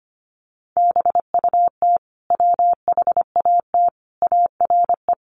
Morse code, heard as a keyed tone, about 25 wpm, 700 Hz. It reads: BUT WHAT ARE
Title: Sentences from Top 100 Words 25wpm